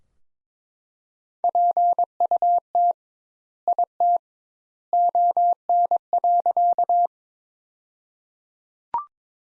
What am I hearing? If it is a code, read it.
Morse code, heard as a keyed tone, about 22 wpm, 700 Hz. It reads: PUT IT ON.
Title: Sentences Top 200 Words - 22wpm